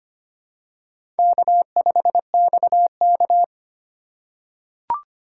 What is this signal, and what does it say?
Morse code, heard as a keyed tone, about 25 wpm, 700 Hz. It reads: K5XK
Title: K1USN SST QSO Practice - 25wpm